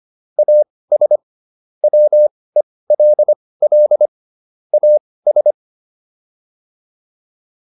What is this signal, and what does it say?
Morse code, heard as a keyed tone, about 25 wpm, spaced as for 18 wpm, 600 Hz. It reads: AS WELL AS